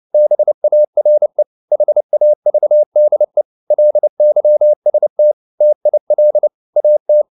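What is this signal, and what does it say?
Morse code, heard as a keyed tone, about 29 wpm, 600 Hz. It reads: BARE HAVDE LYST TIL AT
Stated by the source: Fast (QRQ) Morse at 29 wpm, from CW med Gnister.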